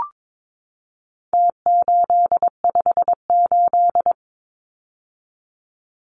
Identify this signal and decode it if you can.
Morse code, heard as a keyed tone, about 22 wpm, 700 Hz. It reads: T858
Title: Numbers - Contest Style - 4-Digit 22wpm